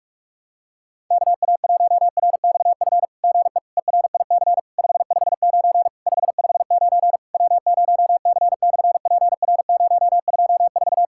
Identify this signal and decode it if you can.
Morse code, heard as a keyed tone, about 46 wpm, 700 Hz. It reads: KA1RXF GE ERIC 559 559 W0C/PR024
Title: SOTA QSO Practice 45wpm